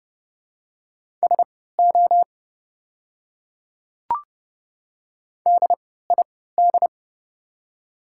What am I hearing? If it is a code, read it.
Morse code, heard as a keyed tone, about 30 wpm, spaced as for 17 wpm, 700 Hz. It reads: SO DID
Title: 3x Character Spacing - Top 300 Words - 30wpm